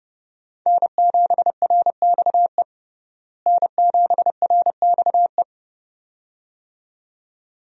Morse code, heard as a keyed tone, about 30 wpm, 700 Hz. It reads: N7RXE N7RXE
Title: USA Call Signs - 5 Characters - Repeated 2x - 30wpm